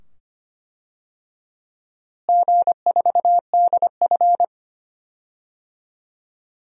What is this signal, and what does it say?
Morse code, heard as a keyed tone, about 25 wpm, 700 Hz. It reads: G4DF
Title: DXCC-100 Call Signs - 4 Characters - Repeated 2x - 25wpm